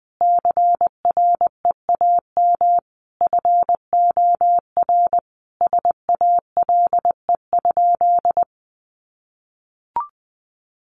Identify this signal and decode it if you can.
Morse code, heard as a keyed tone, about 19 wpm, 700 Hz. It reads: CREAM FOR SALE?
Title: Sentences from Top 1500 Words 20wpm